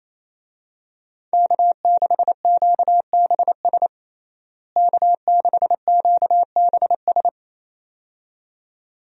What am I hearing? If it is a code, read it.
Morse code, heard as a keyed tone, about 28 wpm, 700 Hz. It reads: K6QBS K6QBS
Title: USA Call Signs - 5 Characters - Repeated 2x - 28wpm